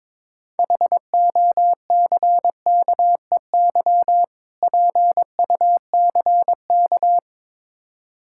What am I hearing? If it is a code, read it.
Morse code, heard as a keyed tone, about 22 wpm, 700 Hz. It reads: HOCKEY PUCK
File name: Difficult Words with Hints - 22wpm